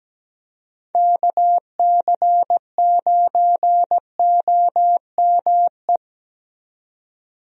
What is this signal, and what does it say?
Morse code, heard as a keyed tone, about 17 wpm, 700 Hz. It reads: KC9OME
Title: USA Call Signs - 6 Characters - 17wpm